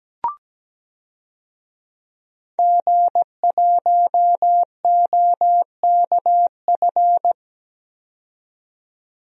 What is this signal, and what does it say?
Morse code, heard as a keyed tone, about 17 wpm, 700 Hz. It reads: G1OKF
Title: DXCC-100 Call Signs - 5 Characters - 17wpm